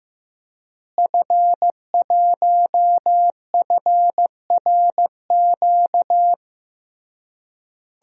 Morse code, heard as a keyed tone, about 15 wpm, 700 Hz. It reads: F1FRQ
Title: DXCC-100 Call Signs - 5 Characters - Repeated 3x - 15wpm